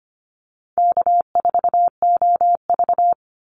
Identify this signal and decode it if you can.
Morse code, heard as a keyed tone, about 25 wpm, 700 Hz. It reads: K4OV
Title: Easy Call Signs - 25wpm